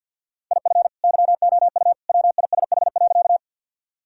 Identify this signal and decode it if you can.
Morse code, heard as a keyed tone, about 51 wpm, 700 Hz. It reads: IF YOU WISH.
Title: Sentences from Top 1600 Words 50wpm